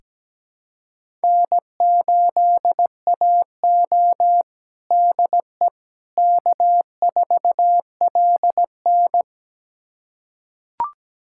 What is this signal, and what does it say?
Morse code, heard as a keyed tone, about 17 wpm, 700 Hz. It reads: N8AO DE K4LN
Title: USA Callsign DE Callsign - 4 Characters - 17wpm